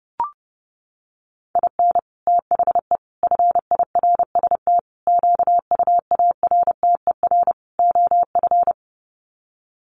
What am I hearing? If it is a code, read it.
Morse code, heard as a keyed tone, about 30 wpm, 700 Hz. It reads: IN THE FIRST QUARTER OF
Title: Sets of 5 Words Encore2 30wpm